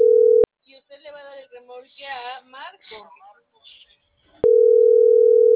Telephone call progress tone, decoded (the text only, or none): ringback tone